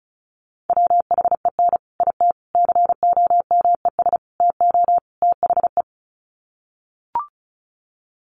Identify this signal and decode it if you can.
Morse code, heard as a keyed tone, about 34 wpm, 700 Hz. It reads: WHEN IT COMES TO THE
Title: Sets of 5 Words 35wpm